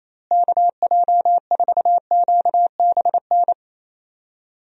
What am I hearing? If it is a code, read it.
Morse code, heard as a keyed tone, about 28 wpm, 700 Hz. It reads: KJ4QBN
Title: USA Call Signs - 6 Characters - Repeated 2x - 28wpm